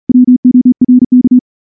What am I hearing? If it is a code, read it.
Morse code, heard as a keyed tone, about 36 wpm, 250 Hz. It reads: WORK
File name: morse / morse-training-40WPM